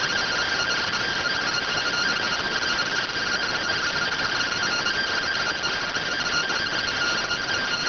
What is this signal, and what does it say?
Two-tone teleprinter signal, about 170 Hz shift.